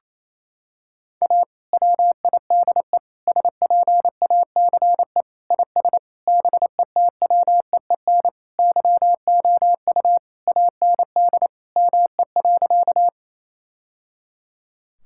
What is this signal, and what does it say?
Morse code, heard as a keyed tone, about 28 wpm, 700 Hz. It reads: A WIDE SPACE IS BETWEEN YOU AND ME.
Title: Sentences from Top 1000 Words - Review Part 6 - 28wpm